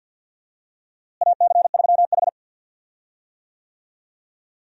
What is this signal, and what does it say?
Morse code, heard as a keyed tone, about 52 wpm, 700 Hz. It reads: AC3H
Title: USA Call Signs - 4 Characters - Repeated 2x - 50wpm